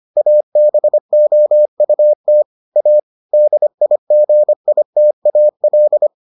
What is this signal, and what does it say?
Morse code, heard as a keyed tone, about 25 wpm, 600 Hz. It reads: ABOUT A DIGITAL